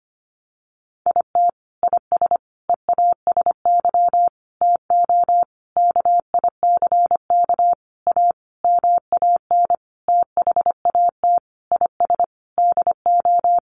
Morse code, heard as a keyed tone, about 25 wpm, 700 Hz. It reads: IT IS EASY TO KICK A MAN THAT IS DO
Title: Sentences from Top 1300 Words 25wpm